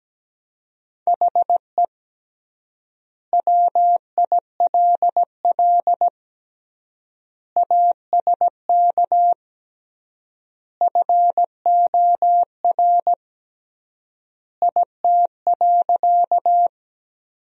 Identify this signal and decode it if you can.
Morse code, heard as a keyed tone, about 17 wpm, 700 Hz. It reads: HE WILL ASK FOR IT.
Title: 3x Word-Spacing - Sentences from Top 200 Words - 17wpm